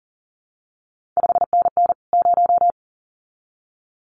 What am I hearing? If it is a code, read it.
Morse code, heard as a keyed tone, about 42 wpm, 700 Hz. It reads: 5NN 0
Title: CQ WW DX Contest Exchange Encore - 40wpm